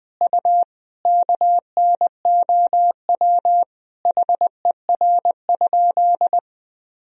Morse code, heard as a keyed tone, about 20 wpm, 700 Hz. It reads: U KNOW HER?
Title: Sentences from Top 100 Words and Abbreviations 20wpm